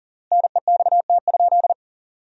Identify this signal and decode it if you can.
Morse code, heard as a keyed tone, about 40 wpm, 700 Hz. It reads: NEXT?
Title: 3x Word-Spacing - Sentences from Top 300 Words - 40wpm